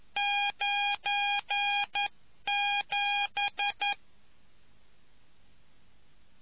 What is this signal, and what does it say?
Morse code, heard as a keyed tone, about 11 wpm, 2450 Hz. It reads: 97